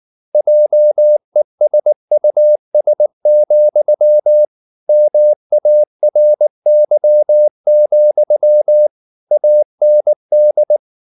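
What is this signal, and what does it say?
Morse code, heard as a keyed tone, about 19 wpm, 600 Hz. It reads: JESUS, MARY, AND
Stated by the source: Morse code at 19 wpm from CW med Gnister, reading Ken Follett's novel Never in English.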